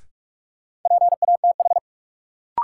Morse code, heard as a keyed tone, about 46 wpm, 700 Hz. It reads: PATH
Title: Sentences from Top 1400 Words - New Word Drill Alt - 45wpm